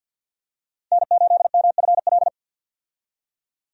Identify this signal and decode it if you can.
Morse code, heard as a keyed tone, about 50 wpm, 700 Hz. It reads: N8MFL
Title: USA Call Signs - 5 Characters - 50wpm